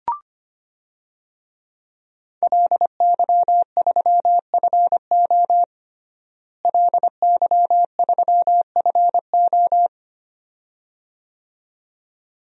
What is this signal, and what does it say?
Morse code, heard as a keyed tone, about 25 wpm, 700 Hz. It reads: LY3FO LY3FO
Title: DXCC-100 Call Signs - 5 Characters - Repeated 2x - 25wpm